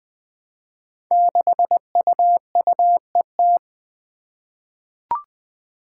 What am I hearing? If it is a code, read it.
Morse code, heard as a keyed tone, about 21 wpm, 700 Hz. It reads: 6UUET
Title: Five Challenging Letter-Number Combinations 20wpm